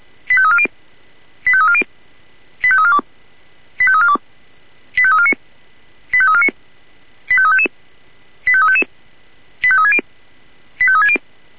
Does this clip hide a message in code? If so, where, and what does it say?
Yes: five-tone selective call (ZVEI) from 0.3 to 11.2 s, heard as quick runs of five tones, tones of about 70 ms. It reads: 85340 85340 85342 85342 85349 85349 85355 85360 85360